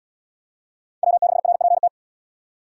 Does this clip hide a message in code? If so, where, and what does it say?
Yes: Morse code from 1.0 to 1.9 s, heard as a keyed tone, about 73 wpm, 700 Hz. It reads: W6RPT